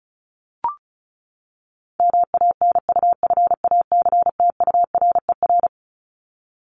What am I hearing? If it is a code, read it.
Morse code, heard as a keyed tone, about 34 wpm, 700 Hz. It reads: MANUFACTURER
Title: Top 1751-2000 Words 35wpm